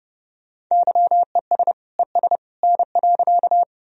Morse code, heard as a keyed tone, about 30 wpm, 700 Hz. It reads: YES ES N.
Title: Sentences from Top 500 Words and Abbreviations 30wpm